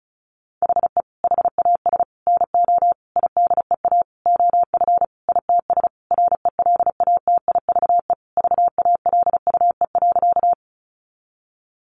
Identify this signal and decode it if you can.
Morse code, heard as a keyed tone, about 34 wpm, 700 Hz. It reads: HE HAS NO IDEA OF ITS RELATIVE VALUE.
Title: Sentences from Top 1400 Words 35wpm